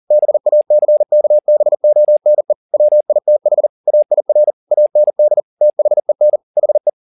Morse code, heard as a keyed tone, about 40 wpm, 600 Hz. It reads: BACKBONE WITH AIR AND THEN HE